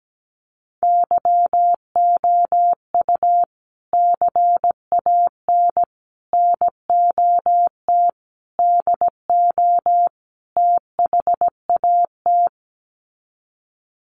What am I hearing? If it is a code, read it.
Morse code, heard as a keyed tone, about 17 wpm, 700 Hz. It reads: YOU CAN NOT DO THAT
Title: Sets of 5 Words 17wpm